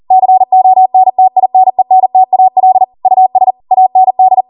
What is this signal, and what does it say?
Morse code, heard as a keyed tone, about 40 wpm, 750 Hz. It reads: CONTINENTAL US AND